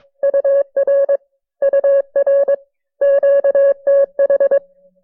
Morse code, heard as a keyed tone, about 20 wpm, 590 Hz. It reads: UR UR QTH
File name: FreeFiles / CW-practice / KB6MT-tape1-side2